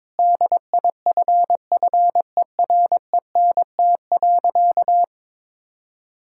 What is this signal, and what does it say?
Morse code, heard as a keyed tone, about 22 wpm, 700 Hz. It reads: DIFFERENT.
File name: practice / 22wpm / Sentences with 4-Word Phrases - 22wpm